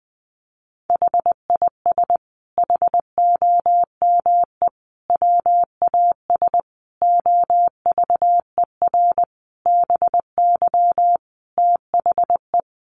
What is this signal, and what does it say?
Morse code, heard as a keyed tone, about 20 wpm, 700 Hz. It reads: HIS HOME WAS OVER BY THE
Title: Sentences from Top 1500 Words 20wpm